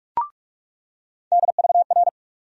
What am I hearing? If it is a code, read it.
Morse code, heard as a keyed tone, about 43 wpm, 700 Hz. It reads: DVR
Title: USA Call Sign Postfixes - 3 Letters 45wpm